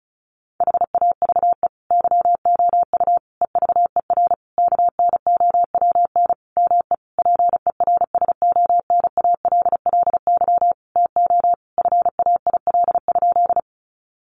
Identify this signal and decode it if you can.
Morse code, heard as a keyed tone, about 34 wpm, 700 Hz. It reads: HAVE YOU EVER KNOWN ME PERSONALLY TO FAIL?
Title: Sentences from Top 1400 Words 35wpm